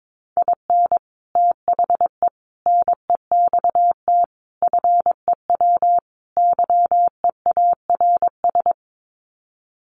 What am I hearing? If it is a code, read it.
Morse code, heard as a keyed tone, about 22 wpm, 700 Hz. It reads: IN THE NEXT FEW YEARS
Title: Sets of 5 Words Encore 22wpm